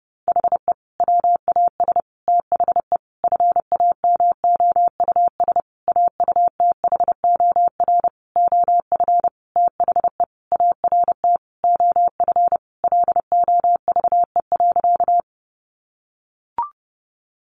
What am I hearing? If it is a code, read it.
Morse code, heard as a keyed tone, about 30 wpm, 700 Hz. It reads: HE WAS THE FAMOUS AUTHOR OF THE ART OF LOVE.
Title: Sentences from Top 1500 Words 30wpm